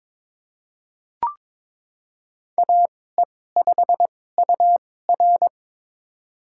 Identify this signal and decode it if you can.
Morse code, heard as a keyed tone, about 22 wpm, spaced as for 16 wpm, 700 Hz. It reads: AE5UR
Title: USA Call Signs - 5 Characters - 2x Character Spacing - 22wpm